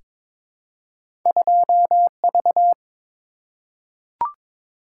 Morse code, heard as a keyed tone, about 22 wpm, 690 Hz. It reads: 2V
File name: practice / 22wpm / Two Letter-Number Combinations - 22wpm